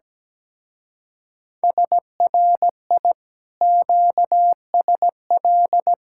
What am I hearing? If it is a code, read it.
Morse code, heard as a keyed tone, about 17 wpm, 700 Hz. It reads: SRI QSL
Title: Speed-Racing Common QSO Phrases - 17wpm